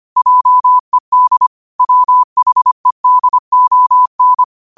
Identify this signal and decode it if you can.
Morse code, heard as a keyed tone, about 25 wpm, 1000 Hz. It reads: JED WHEDON
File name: qotdfiles / 25wpm